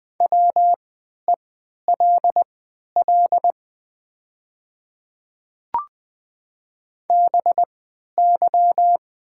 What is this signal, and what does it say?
Morse code, heard as a keyed tone, about 20 wpm, spaced as for 11 wpm, 700 Hz. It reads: WELL BY
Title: 3x Character Spacing - Top 200 Words - 20wpm